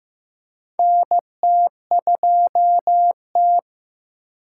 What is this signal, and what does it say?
Morse code, heard as a keyed tone, about 15 wpm, 700 Hz. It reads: NT2T
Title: USA Call Signs - 4 Characters - 8x Character Spacing - 15wpm